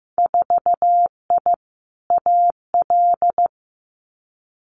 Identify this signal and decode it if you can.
Morse code, heard as a keyed tone, about 15 wpm, 700 Hz. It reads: 4I AL
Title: Winter Field Day Exchanges Encore 15wpm